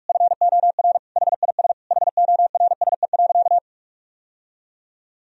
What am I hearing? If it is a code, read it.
Morse code, heard as a keyed tone, about 46 wpm, 700 Hz. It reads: FOR HIS HORSE.
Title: Sentences from Top 1000 Words - Review Part 3 - 45wpm